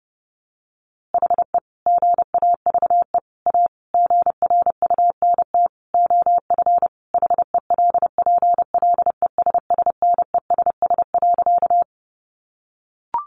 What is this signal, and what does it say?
Morse code, heard as a keyed tone, about 30 wpm, 700 Hz. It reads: HE GAVE A GRUNT OF HELPLESSNESS.
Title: A Sound Of Thunder 30wpm